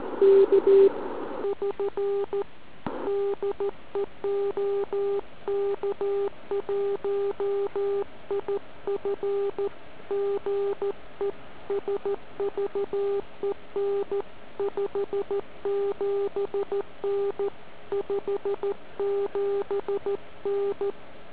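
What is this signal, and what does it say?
Morse code, heard as a keyed tone, about 14 wpm, 400 Hz. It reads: K <SN> DE OK1IF GE SVEN 57N 57N
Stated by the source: SM7ALC heard via tropo on 144 MHz with an FT817 and a quarter-wave antenna on a central-heating radiator.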